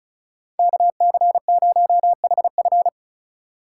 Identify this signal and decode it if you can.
Morse code, heard as a keyed tone, about 44 wpm, 700 Hz. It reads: KC0HF